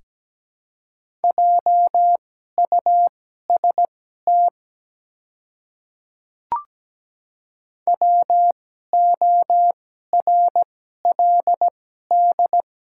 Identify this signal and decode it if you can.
Morse code, heard as a keyed tone, about 17 wpm, spaced as for 12 wpm, 700 Hz. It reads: JUST WORLD
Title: 2x Character Spacing - Top 200 Words - 17wpm